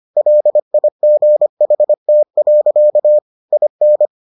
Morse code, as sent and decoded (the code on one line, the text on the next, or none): .-.. .. --. .... - .-.-.- / .. -.
LIGHT. IN